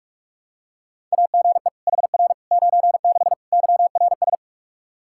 Morse code, heard as a keyed tone, about 46 wpm, 700 Hz. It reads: AGE HR 96 YRS
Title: K1USN SST QSO Practice 45wpm